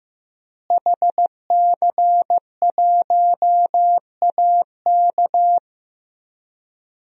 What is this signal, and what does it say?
Morse code, heard as a keyed tone, about 15 wpm, 700 Hz. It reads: HC1AK
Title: DXCC-100 Call Signs - 5 Characters - Repeated 2x - 15wpm